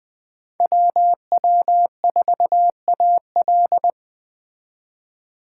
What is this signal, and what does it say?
Morse code, heard as a keyed tone, about 20 wpm, 700 Hz. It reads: WW4AL